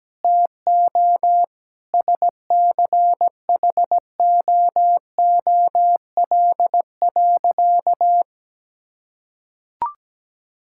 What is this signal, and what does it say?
Morse code, heard as a keyed tone, about 17 wpm, 700 Hz. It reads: TO SCHOOL.
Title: Sentences from Top 1000 Words - Review Part 6 - 17wpm